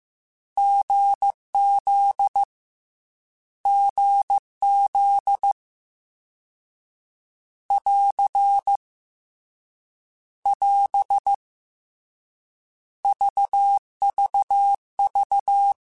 Morse code, heard as a keyed tone, about 15 wpm, 800 Hz. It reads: GZ GZ <AR> <AS> VVV